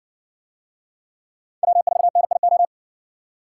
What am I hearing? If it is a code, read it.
Morse code, heard as a keyed tone, about 60 wpm, 700 Hz. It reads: W4NIO